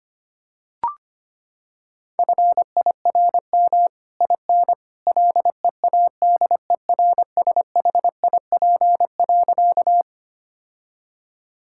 Morse code, heard as a keyed tone, about 25 wpm, 700 Hz. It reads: FIRM IN LEADERSHIP.
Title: Sentences from Top 1600 Words 25wpm